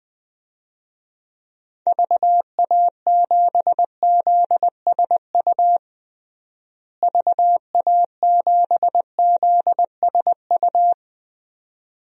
Morse code, heard as a keyed tone, about 20 wpm, 700 Hz. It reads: VA7ZSU VA7ZSU
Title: DXCC-100 Call Signs - 6 Characters - Repeated 2x - 20wpm